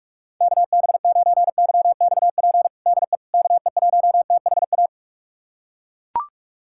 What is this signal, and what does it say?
Morse code, heard as a keyed tone, about 42 wpm, 700 Hz. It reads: KB9YXP DE KE1THA
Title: USA Callsign DE Callsign 45wpm